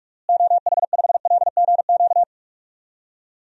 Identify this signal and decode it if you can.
Morse code, heard as a keyed tone, about 46 wpm, 700 Hz. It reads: OH5LGQ
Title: DXCC-100 Call Signs - 6 Characters - Repeated 2x - 45wpm